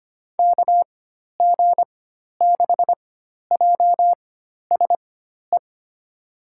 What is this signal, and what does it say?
Morse code, heard as a keyed tone, about 25 wpm, spaced as for 12 wpm, 700 Hz. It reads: KG6JSE
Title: USA Call Signs - 6 Characters - 4x Character Spacing - 25wpm